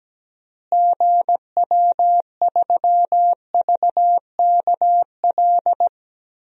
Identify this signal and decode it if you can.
Morse code, heard as a keyed tone, about 17 wpm, 700 Hz. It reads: GW3VKL
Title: DXCC-100 Call Signs - 6 Characters - Repeated 3x - 17wpm